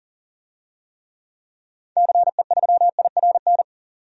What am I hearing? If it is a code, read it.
Morse code, heard as a keyed tone, about 40 wpm, 700 Hz. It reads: CE3IRN
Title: DXCC-100 Call Signs - 6 Characters - 40wpm